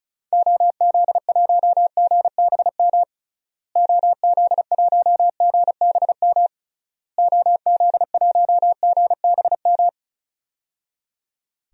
Morse code, heard as a keyed tone, about 36 wpm, 700 Hz. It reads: OZ1GBM OZ1GBM OZ1GBM
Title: DXCC-100 Call Signs - 6 Characters - Repeated 3x - 35wpm